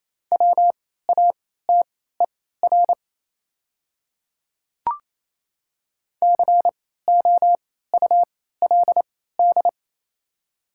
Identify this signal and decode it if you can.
Morse code, heard as a keyed tone, about 28 wpm, spaced as for 16 wpm, 700 Hz. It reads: WATER COULD
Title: 3x Character Spacing - Top 100 Words - 28wpm